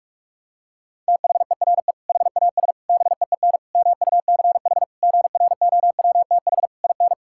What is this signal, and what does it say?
Morse code, heard as a keyed tone, about 46 wpm, 700 Hz. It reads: THERE HAS BEEN MUCH GROWTH IN
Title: Sentences from Top 1000 Words - Review Part 5 - 45wpm